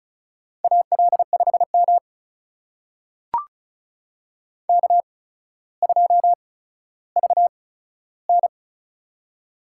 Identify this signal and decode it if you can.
Morse code, heard as a keyed tone, about 36 wpm, spaced as for 15 wpm, 700 Hz. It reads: AL5M K2VN
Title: USA Call Signs - 4 Characters - 8x Character Spacing - 35wpm